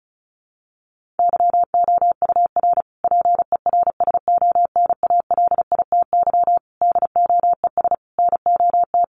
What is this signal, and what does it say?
Morse code, heard as a keyed tone, about 34 wpm, 700 Hz. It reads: YOUR PERSONALITY DOES NOT
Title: Sentences from Top 1400 Words 35wpm